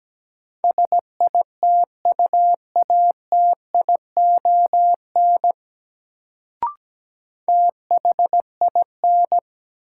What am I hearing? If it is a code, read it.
Morse code, heard as a keyed tone, about 17 wpm, 700 Hz. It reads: SITUATION THIN
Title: Sentences from Top 1200 Words - New Word Drill - 17wpm